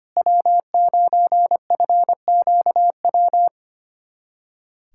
Morse code, heard as a keyed tone, about 25 wpm, 700 Hz. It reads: W9FQW
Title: USA Callsign DE Callsign - 5 Characters - 25wpm